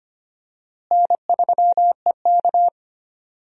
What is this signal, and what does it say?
Morse code, heard as a keyed tone, about 25 wpm, 700 Hz. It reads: N3EK
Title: USA Call Signs - 4 Characters - Repeated 2x - 25wpm